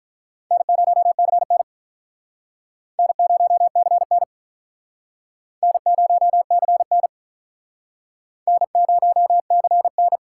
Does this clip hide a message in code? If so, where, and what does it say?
Yes: Morse code from 0.5 to 10.2 s, heard as a keyed tone, about 45 wpm, 700 Hz. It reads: N0CN N0CN N0CN N0CN